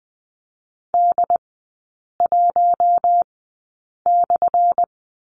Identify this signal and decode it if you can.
Morse code, heard as a keyed tone, about 19 wpm, spaced as for 8 wpm, 690 Hz. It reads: D1/
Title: 198 Lesson 32e - Letter k - Three Characters 20wpm